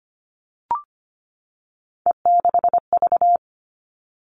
Morse code, heard as a keyed tone, about 25 wpm, 700 Hz. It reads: E6V
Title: All Cut Numbers - 3-Digit 25wpm